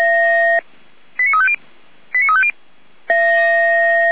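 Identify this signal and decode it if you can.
Five-tone selective call (ZVEI), heard as quick runs of five tones, tones of about 70 ms. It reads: 89350 89350
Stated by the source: ZVEI five-tone selective call received by SDR.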